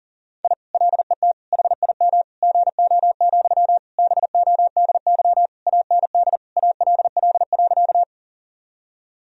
Morse code, heard as a keyed tone, about 40 wpm, 700 Hz. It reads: I LET HIM GO, BODY AND ALL.
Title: Sentences from Top 1000 Words - Review Part 2 - 40wpm